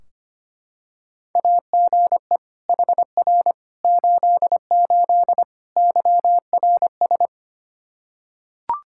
Morse code, heard as a keyed tone, about 25 wpm, 700 Hz. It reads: AGE HR 88 YRS
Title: Practice QSO - Exchange 3 - More Info - 25wpm